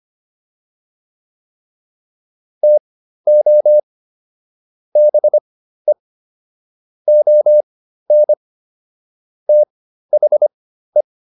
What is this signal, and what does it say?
Morse code, heard as a keyed tone, about 25 wpm, spaced as for 13 wpm, 600 Hz. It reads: TO BE ON THE